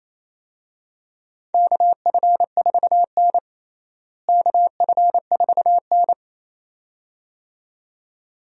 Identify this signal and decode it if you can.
Morse code, heard as a keyed tone, about 28 wpm, 700 Hz. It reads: KF4N KF4N
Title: USA Call Signs - 4 Characters - Repeated 2x - 28wpm